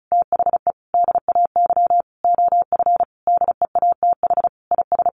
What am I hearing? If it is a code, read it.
Morse code, heard as a keyed tone, about 34 wpm, 700 Hz. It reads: THE DAY OF DEATH IS
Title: Sentences from Top 1100 Words 35wpm